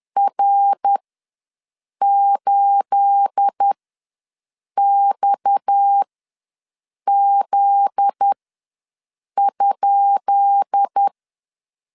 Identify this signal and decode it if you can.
Morse code, heard as a keyed tone, about 10 wpm, spaced as for 6 wpm, 800 Hz. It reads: R8XZ?
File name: LEZIONI / 10 SIGNS / 10B INT ws 10wpm 4134